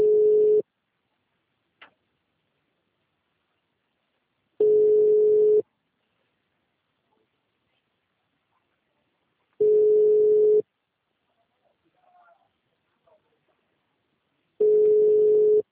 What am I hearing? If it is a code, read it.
Ringback tone (Europe, 425 Hz).